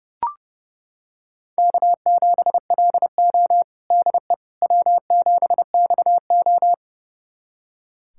Morse code, heard as a keyed tone, about 30 wpm, 700 Hz. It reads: K7LO DE W7XO
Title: USA Callsign DE Callsign - 4 Characters - 30wpm